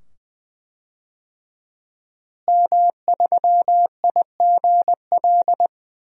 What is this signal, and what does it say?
Morse code, heard as a keyed tone, about 20 wpm, 700 Hz. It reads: M3IGL
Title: DXCC-100 Call Signs - 5 Characters - Repeated 2x - 20wpm